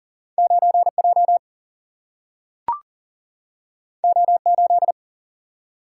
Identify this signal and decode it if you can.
Morse code, heard as a keyed tone, about 40 wpm, 700 Hz. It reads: 9J O8
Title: Single-Letter Single-Number 40wpm